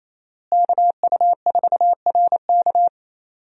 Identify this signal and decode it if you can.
Morse code, heard as a keyed tone, about 27 wpm, 700 Hz. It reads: KU4RK
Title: USA Call Signs - 5 Characters - Repeated 2x - 28wpm